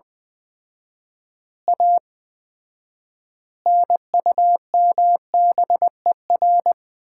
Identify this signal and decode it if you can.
Morse code, heard as a keyed tone, about 20 wpm, 700 Hz. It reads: A NUMBER
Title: 4x Word-Spacing - Sets of 5 Words - 20wpm